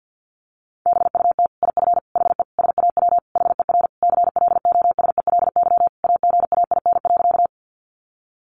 Morse code, heard as a keyed tone, about 50 wpm, 700 Hz. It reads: BUT IF HE SAW HER CLOSELY AGAIN.
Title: Sentences from Top 1400 Words 50wpm